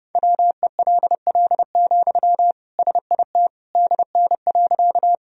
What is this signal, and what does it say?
Morse code, heard as a keyed tone, about 32 wpm, 700 Hz. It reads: WELL, SIT DN.
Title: Sentences from Top 500 Words and Abbreviations 30wpm